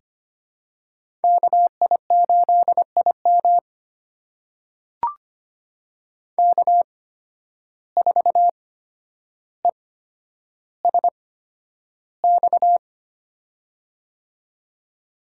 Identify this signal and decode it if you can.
Morse code, heard as a keyed tone, about 25 wpm, spaced as for 11 wpm, 700 Hz. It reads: KI8IM K4ESX
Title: USA Call Signs - 5 Characters - 8x Character Spacing - 25wpm